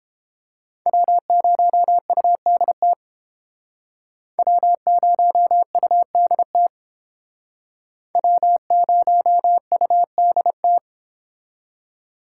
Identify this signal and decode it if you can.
Morse code, heard as a keyed tone, about 29 wpm, 700 Hz. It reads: W0UDT W0UDT W0UDT